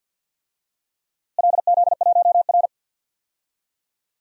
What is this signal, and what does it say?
Morse code, heard as a keyed tone, about 48 wpm, 700 Hz. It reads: LZ1R
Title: DXCC-100 Call Signs - 4 Characters - Repeated 2x - 50wpm